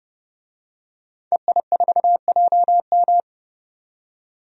Morse code, heard as a keyed tone, about 30 wpm, 700 Hz. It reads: EI4JM
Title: DXCC-100 Call Signs - 5 Characters - 30wpm